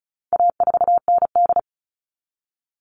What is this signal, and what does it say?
Morse code, heard as a keyed tone, about 34 wpm, 700 Hz. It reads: A4ND
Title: All Cut Numbers - 4-Digit 35wpm